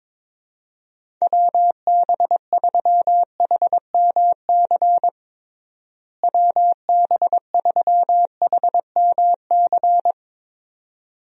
Morse code, heard as a keyed tone, about 22 wpm, 700 Hz. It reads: WB3HMC WB3HMC
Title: USA Call Signs - 6 Characters - Repeated 2x - 22wpm